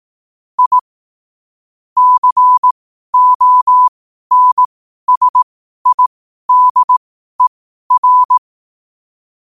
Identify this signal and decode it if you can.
Morse code, heard as a keyed tone, about 18 wpm, spaced as for 12 wpm, 1000 Hz. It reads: I CONSIDER
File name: qotdfiles / 12wpm